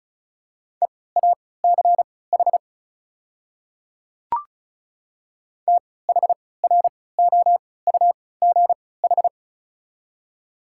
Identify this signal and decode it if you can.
Morse code, heard as a keyed tone, about 36 wpm, spaced as for 20 wpm, 700 Hz. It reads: EACH THROUGH
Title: 3x Character Spacing - Top 200 Words - 35wpm